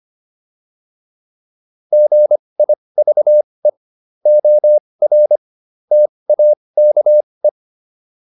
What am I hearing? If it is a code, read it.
Morse code, heard as a keyed tone, about 25 wpm, spaced as for 20 wpm, 600 Hz. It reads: GIVE OR TAKE